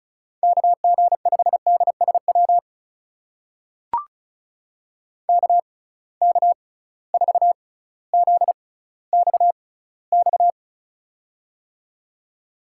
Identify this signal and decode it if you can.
Morse code, heard as a keyed tone, about 36 wpm, spaced as for 18 wpm, 700 Hz. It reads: KG5DSW KK4ZXX
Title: USA Call Signs - 6 Characters - 6x Character Spacing - 35wpm